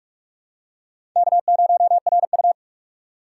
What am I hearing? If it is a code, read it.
Morse code, heard as a keyed tone, about 46 wpm, 700 Hz. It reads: K0RU
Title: USA Call Signs - 4 Characters - 6x Character Spacing - 45wpm